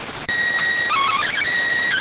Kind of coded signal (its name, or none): SSTV (slow-scan television)